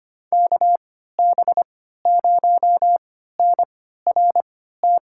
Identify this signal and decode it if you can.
Morse code, heard as a keyed tone, about 25 wpm, spaced as for 14 wpm, 700 Hz. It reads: KB0NRT
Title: USA Call Signs - 6 Characters - 3x Character Spacing - 25wpm